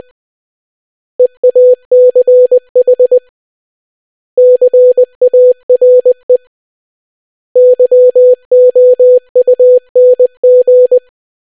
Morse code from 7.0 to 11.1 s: YOUNG